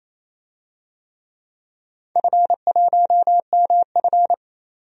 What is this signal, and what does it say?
Morse code, heard as a keyed tone, about 28 wpm, 700 Hz. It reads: F1MF
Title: DXCC-100 Call Signs - 4 Characters - 28wpm